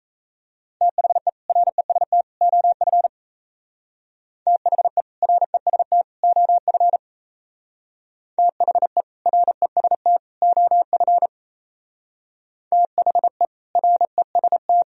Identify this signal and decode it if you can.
Morse code, heard as a keyed tone, about 34 wpm, 700 Hz. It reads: THE REST OF THE REST OF THE REST OF THE REST
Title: Speed-Racing - Sets of 3 Words 28wpm